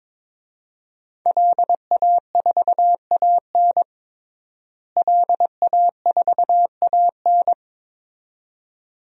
Morse code, heard as a keyed tone, about 22 wpm, 700 Hz. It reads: LA4AN LA4AN
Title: DXCC-100 Call Signs - 5 Characters - Repeated 2x - 22wpm